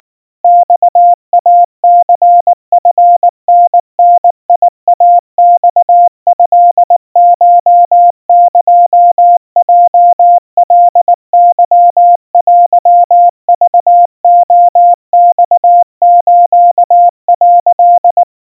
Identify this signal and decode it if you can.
Morse code (Japanese Wabun code), heard as a keyed tone, about 19 wpm, 700 Hz. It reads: マイニチタダイマトコエヲカケテクレマス」
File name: kawa / Morse sample / kamyu 70 cpm 700 Hz Crispy